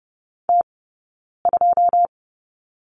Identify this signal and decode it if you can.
Morse code, heard as a keyed tone, about 30 wpm, 700 Hz. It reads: T 2
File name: Lesson 20d - Number 3 - Two Characters 30wpm